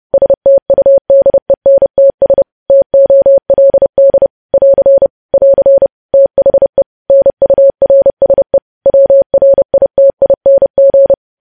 Morse code, heard as a keyed tone, about 30 wpm, 550 Hz. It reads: STUDENTS TOLD <AR> <AR> THE NURSE WRITING
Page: news headlines in morse code at 30 wpm